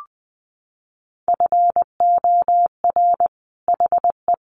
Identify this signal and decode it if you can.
Morse code, heard as a keyed tone, about 20 wpm, 700 Hz. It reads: FOR HE
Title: Sentences from Top 1400 Words 20wpm